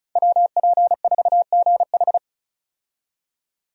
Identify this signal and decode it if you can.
Morse code, heard as a keyed tone, about 34 wpm, 700 Hz. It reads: WP4GH
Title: USA Call Signs - 5 Characters - 35wpm